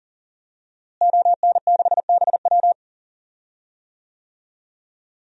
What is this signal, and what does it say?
Morse code, heard as a keyed tone, about 40 wpm, 700 Hz. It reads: ON6BW
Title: DXCC-100 Call Signs - 5 Characters - Repeated 2x - 40wpm